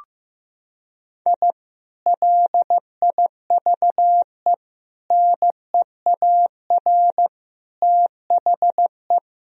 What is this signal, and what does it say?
Morse code, heard as a keyed tone, about 15 wpm, 700 Hz. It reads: I LIVE NEAR THE
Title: Sentences from Top 1000 Words - Review Part 3 - 15wpm